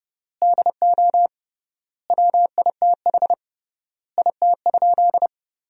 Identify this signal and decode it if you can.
Morse code, heard as a keyed tone, about 30 wpm, 700 Hz. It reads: DO WITH IT?
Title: 3x Word-Spacing - Sentences from Top 100 Words - 30wpm